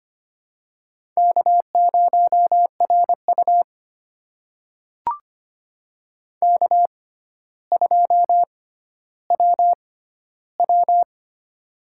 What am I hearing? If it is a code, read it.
Morse code, heard as a keyed tone, about 25 wpm, spaced as for 13 wpm, 700 Hz. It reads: K0RU K2WW
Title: USA Call Signs - 4 Characters - 6x Character Spacing - 25wpm